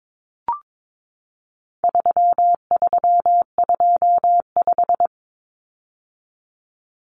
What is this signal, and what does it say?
Morse code, heard as a keyed tone, about 22 wpm, 700 Hz. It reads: 3325
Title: Common Cut Numbers - 4-Digit 22wpm